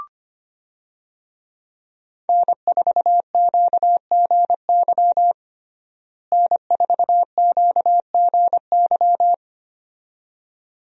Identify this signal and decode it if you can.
Morse code, heard as a keyed tone, about 25 wpm, 700 Hz. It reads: N4QGY N4QGY
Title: USA Call Signs - 5 Characters - Repeated 2x - 25wpm